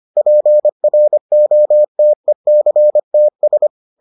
Morse code, heard as a keyed tone, about 25 wpm, 600 Hz. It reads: PROTECTS